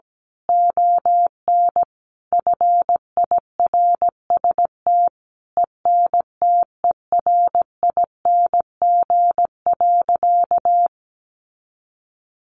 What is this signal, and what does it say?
Morse code, heard as a keyed tone, about 17 wpm, 700 Hz. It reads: ON FIRST ENTERING.
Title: A Sound Of Thunder 17wpm